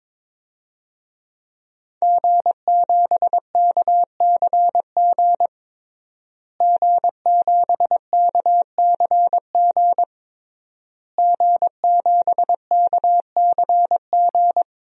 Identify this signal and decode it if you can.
Morse code, heard as a keyed tone, about 22 wpm, 700 Hz. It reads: G7KCG G7KCG G7KCG
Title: DXCC-100 Call Signs - 5 Characters - Repeated 3x - 22wpm